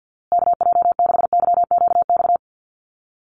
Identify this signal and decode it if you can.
Morse code, heard as a keyed tone, about 50 wpm, 700 Hz. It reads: KJ6YQX
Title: USA Call Signs Repeated 3x 50wpm